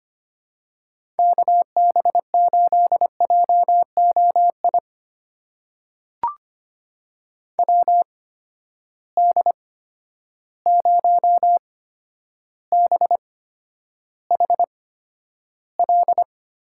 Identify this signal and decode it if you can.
Morse code, heard as a keyed tone, about 25 wpm, spaced as for 11 wpm, 700 Hz. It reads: KB8JOI WD0BHL
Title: USA Call Signs - 6 Characters - 8x Character Spacing - 25wpm